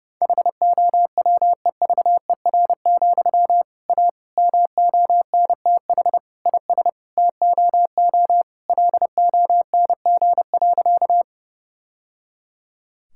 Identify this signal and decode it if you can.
Morse code, heard as a keyed tone, about 30 wpm, 700 Hz. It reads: HOWEVER, A MONTH IS TOO LONG.
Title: Sentences from Top 1000 Words - Review Part 5 - 30wpm